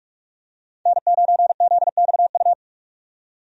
Morse code, heard as a keyed tone, about 46 wpm, 700 Hz. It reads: N9ZXU
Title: USA Call Signs - 5 Characters - 45wpm